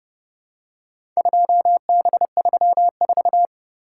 Morse code, heard as a keyed tone, about 29 wpm, 700 Hz. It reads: 2B34